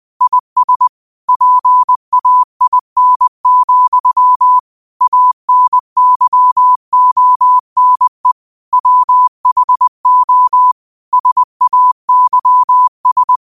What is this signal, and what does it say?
Morse code, heard as a keyed tone, about 20 wpm, 1000 Hz. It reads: IS PAIN, ANYONE WHO SAYS
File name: qotdfiles / 20wpm